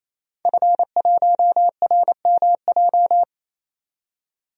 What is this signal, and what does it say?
Morse code, heard as a keyed tone, about 27 wpm, 700 Hz. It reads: F1RMJ
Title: DXCC-100 Call Signs - 5 Characters - 28wpm